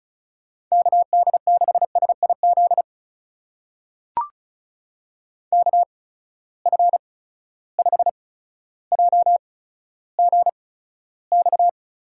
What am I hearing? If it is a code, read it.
Morse code, heard as a keyed tone, about 36 wpm, spaced as for 15 wpm, 700 Hz. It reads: KD6SIZ KF5JGX
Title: USA Call Signs - 6 Characters - 8x Character Spacing - 35wpm